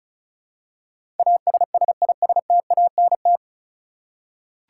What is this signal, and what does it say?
Morse code, heard as a keyed tone, about 36 wpm, 700 Hz. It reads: ASSISTANT
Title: Practice QSO - Exchange 3 - More Info - 35wpm